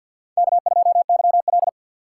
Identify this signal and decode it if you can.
Morse code, heard as a keyed tone, about 53 wpm, 700 Hz. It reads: K2YL
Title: USA Call Signs - 3x Character Spacing - 50wpm